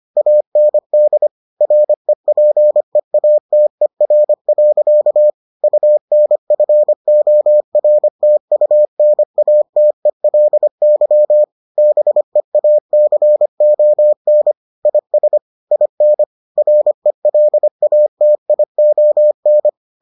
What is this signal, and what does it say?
Morse code, heard as a keyed tone, about 25 wpm, 600 Hz. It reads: AND REPEATER. UNFORTUNATELY BEACON IS IN RELATION